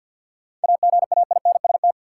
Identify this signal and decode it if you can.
Morse code, heard as a keyed tone, about 51 wpm, 700 Hz. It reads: AGAINST
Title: Sentences from Top 400 Words - New Word Drill - 50wpm